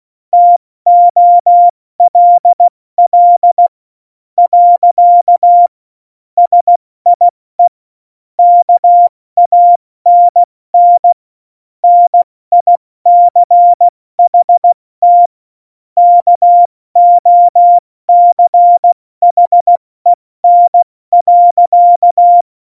Morse code, heard as a keyed tone, about 16 wpm, spaced as for 14 wpm, 700 Hz. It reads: TOLL. SIE KANN NICHT KOCHEN.
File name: Morsekurs / Witze / Witz 02. Juli 2019, 70 F80